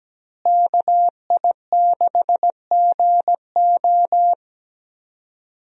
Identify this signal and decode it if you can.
Morse code, heard as a keyed tone, about 17 wpm, 700 Hz. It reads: KI6GO
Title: USA Call Signs - 5 Characters - Repeated 2x - 17wpm